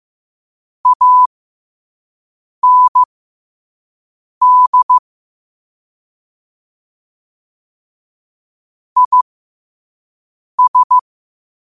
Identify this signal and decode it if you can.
Morse code, heard as a keyed tone, about 15 wpm, spaced as for 5 wpm, 1000 Hz. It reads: AND IS
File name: qotdfiles / 5wpm